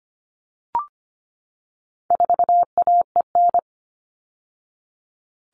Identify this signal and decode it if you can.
Morse code, heard as a keyed tone, about 25 wpm, 700 Hz. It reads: 4AEN
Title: All Cut Numbers - 4-Digit 25wpm